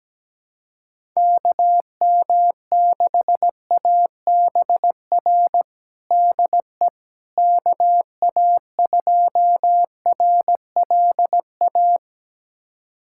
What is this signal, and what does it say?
Morse code, heard as a keyed tone, about 17 wpm, 700 Hz. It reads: KM6ABR DE KA2RLA
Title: USA Callsign DE Callsign 17wpm